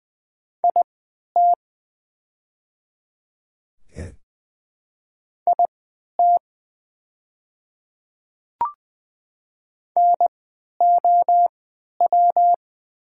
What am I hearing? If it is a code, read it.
Morse code, heard as a keyed tone, about 20 wpm, spaced as for 11 wpm, 700 Hz. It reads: IT IT NOW
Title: 3x Character Spacing - Top 200 Words - 20wpm